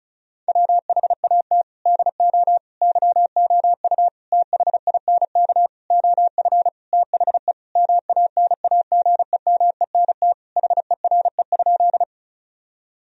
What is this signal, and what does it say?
Morse code, heard as a keyed tone, about 34 wpm, 700 Hz. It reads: WHAT DO YOU THINK OF THE MANAGEMENT HERE?
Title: Sentences from Top 1000 Words - Review Part 4 - 35wpm